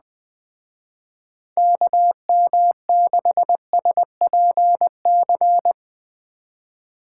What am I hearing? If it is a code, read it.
Morse code, heard as a keyed tone, about 20 wpm, 700 Hz. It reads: KM6SPC
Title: Speed Racing - USA Call Signs 17wpm